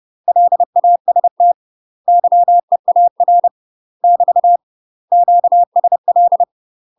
Morse code, heard as a keyed tone, about 30 wpm, 700 Hz. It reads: LAST YEAR <BT> QSL
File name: UK MASTER QSOs TRAINING / 30WPM / UK MASTER QSOs 0050